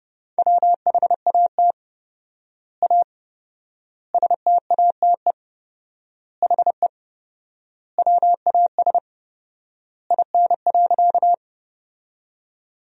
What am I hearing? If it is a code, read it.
Morse code, heard as a keyed tone, about 30 wpm, 700 Hz. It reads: WHAT A STATE HE WAS IN.
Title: 4x Word-Spacing - Sentences from Top 400 Words - 30wpm